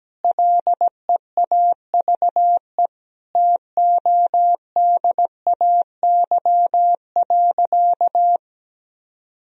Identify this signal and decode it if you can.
Morse code, heard as a keyed tone, about 17 wpm, 700 Hz. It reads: LEAVE TODAY.
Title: Sentences from Top 1000 Words - Review Part 3 - 17wpm